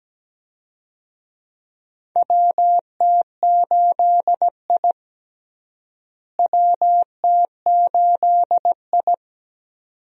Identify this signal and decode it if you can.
Morse code, heard as a keyed tone, about 17 wpm, 700 Hz. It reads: WT8I WT8I
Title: USA Call Signs - 4 Characters - Repeated 2x - 17wpm